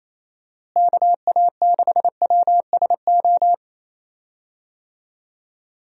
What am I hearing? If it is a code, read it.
Morse code, heard as a keyed tone, about 28 wpm, 700 Hz. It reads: KA6WSO
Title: USA Call Signs - 6 Characters - Repeated 2x - 28wpm